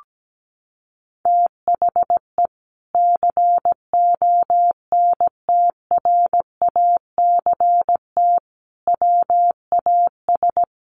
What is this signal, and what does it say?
Morse code, heard as a keyed tone, about 17 wpm, 700 Hz. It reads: THE CONTRACT WAS
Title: Sentences from Top 1400 Words 17wpm